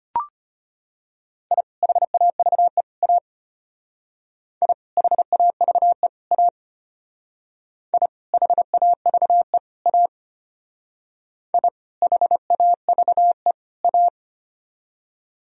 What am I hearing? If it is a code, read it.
Morse code, heard as a keyed tone, about 32 wpm, 700 Hz. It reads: I HAVE A I HAVE A I HAVE A I HAVE A